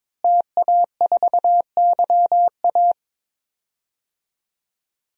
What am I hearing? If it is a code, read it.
Morse code, heard as a keyed tone, about 22 wpm, 700 Hz. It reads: TA4YA
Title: DXCC-100 Call Signs - 5 Characters - Repeated 3x - 22wpm